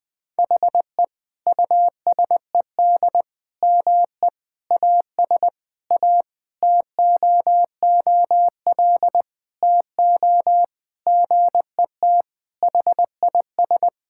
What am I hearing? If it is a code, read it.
Morse code, heard as a keyed tone, about 20 wpm, 700 Hz. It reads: HE USED ME AS A TOOL TO GET HIS
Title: Sentences from Top 1000 Words - Review Part 6 - 20wpm